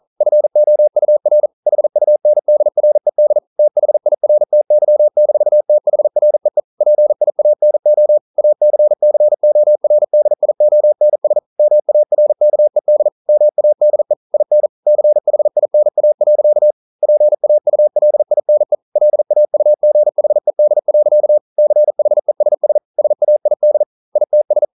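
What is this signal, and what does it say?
Morse code, heard as a keyed tone, about 42 wpm, 600 Hz. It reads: FOUR HUNDRED THIRTY-THREE PIANO ACCORDIONS MARKED MADE IN CHINA. PAULINE LAUGHED. CHESS SAID ITS
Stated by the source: Morse code practice at 41 wpm, from CW med Gnister.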